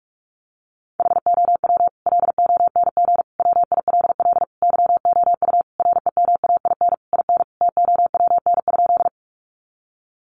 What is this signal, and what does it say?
Morse code, heard as a keyed tone, about 44 wpm, 700 Hz. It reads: HOW LONG WILL YOU REMAIN IN TOWN?
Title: Sentences from Top 1200 Words 45wpm